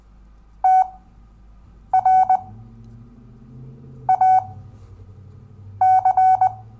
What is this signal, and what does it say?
Morse code, heard as a keyed tone, about 20 wpm, spaced as for 6 wpm, 750 Hz. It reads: TRAC
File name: cw / Lesson10 REVIEW